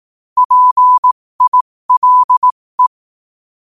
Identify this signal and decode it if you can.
Morse code, heard as a keyed tone, about 18 wpm, spaced as for 15 wpm, 1000 Hz. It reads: PILE
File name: qotdfiles / 15wpm